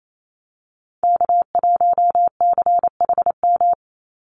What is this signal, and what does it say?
Morse code, heard as a keyed tone, about 28 wpm, 700 Hz. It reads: K1CHM
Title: Call Signs Repeated 2x - 28wpm